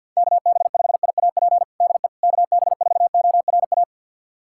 Morse code, heard as a keyed tone, about 48 wpm, 700 Hz. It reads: KB5IAP DE KB4ORA
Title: USA Callsign DE Callsign - 6 Characters - 50wpm